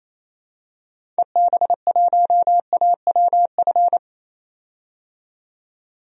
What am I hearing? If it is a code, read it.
Morse code, heard as a keyed tone, about 27 wpm, 700 Hz. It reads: EB1AWF
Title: DXCC-100 Call Signs - 6 Characters - 28wpm